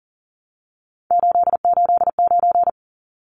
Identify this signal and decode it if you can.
Morse code, heard as a keyed tone, about 40 wpm, 700 Hz. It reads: 889
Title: Numbers - 3-Digit 40wpm